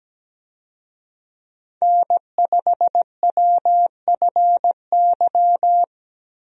Morse code, heard as a keyed tone, about 17 wpm, 700 Hz. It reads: N5WFY
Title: USA Call Signs - 5 Characters - Repeated 3x - 17wpm